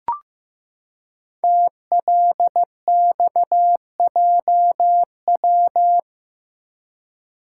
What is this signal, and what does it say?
Morse code, heard as a keyed tone, about 15 wpm, 700 Hz. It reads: TLXJW
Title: Code Groups - 5 Letters - 15wpm